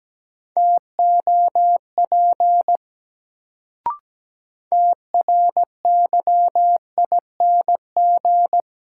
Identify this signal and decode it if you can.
Morse code, heard as a keyed tone, about 17 wpm, 700 Hz. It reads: TOP TRYING
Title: Sets of 2 Words - New Word Drill - 17wpm